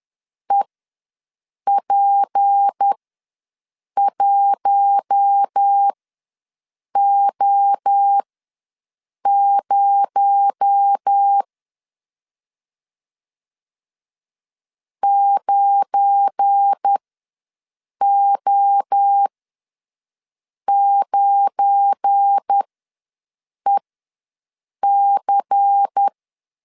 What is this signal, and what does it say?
Morse code, heard as a keyed tone, about 10 wpm, spaced as for 6 wpm, 800 Hz. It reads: EP1O0 9O9EC
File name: LEZIONI / 7 GF9Z / 7 GF9 ws 10wpm 4034